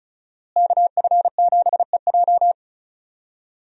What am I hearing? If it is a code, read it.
Morse code, heard as a keyed tone, about 34 wpm, 700 Hz. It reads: KF7EJ
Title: USA Call Signs - 5 Characters - Repeated 3x - 35wpm